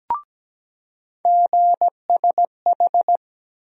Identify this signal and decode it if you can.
Morse code, heard as a keyed tone, about 17 wpm, 700 Hz. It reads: GSH
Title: USA Call Sign Postfixes - 3 Letters 17wpm